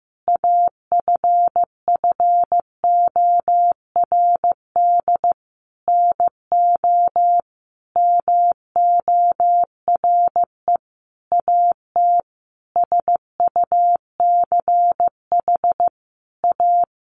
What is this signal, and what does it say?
Morse code, heard as a keyed tone, about 15 wpm, 700 Hz. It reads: AFFORD NO MORE AT SUCH A
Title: Sentences from Top 1100 Words 15wpm